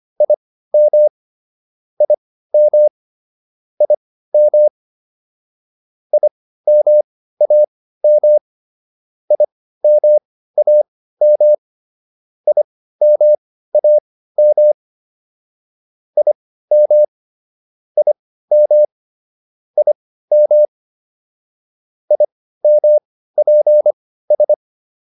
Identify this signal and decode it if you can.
Morse code, heard as a keyed tone, about 25 wpm, spaced as for 15 wpm, 600 Hz. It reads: IM IM IM IMAM IMAM IMAM IM IM IM IMPS
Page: intermediate-practice-files